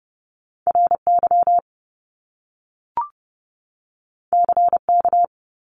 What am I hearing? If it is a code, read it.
Morse code, heard as a keyed tone, about 30 wpm, 700 Hz. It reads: RY CK
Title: Top Two Letter Combinations in Words 30wpm